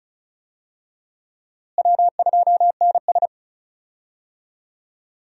Morse code, heard as a keyed tone, about 36 wpm, 700 Hz. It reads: W2NS
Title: USA Call Signs - 4 Characters - Repeated 3x - 35wpm